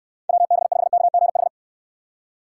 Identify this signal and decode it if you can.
Morse code, heard as a keyed tone, about 69 wpm, 700 Hz. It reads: WB5KGH